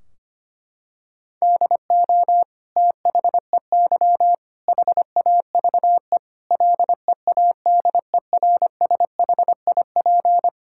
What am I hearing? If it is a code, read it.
Morse code, heard as a keyed tone, about 25 wpm, 700 Hz. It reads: DO THEY HAVE LEADERSHIP
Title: Sentences from Top 1600 Words 25wpm